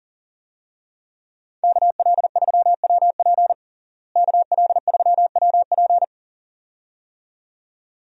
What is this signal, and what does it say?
Morse code, heard as a keyed tone, about 40 wpm, 700 Hz. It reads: KL3WP KL3WP
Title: USA Call Signs - 5 Characters - Repeated 2x - 40wpm